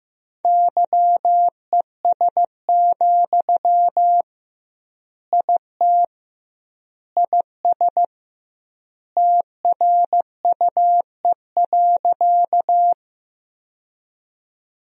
Morse code, heard as a keyed tone, about 15 wpm, 700 Hz. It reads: YES, IT IS TRUE.
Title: 2x Word-Spacing - Sentences from Top 500 Words - 15wpm